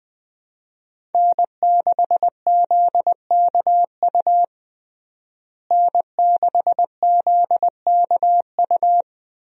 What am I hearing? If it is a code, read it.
Morse code, heard as a keyed tone, about 21 wpm, 700 Hz. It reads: N6ZKU N6ZKU
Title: USA Call Signs - 5 Characters - Repeated 2x - 20wpm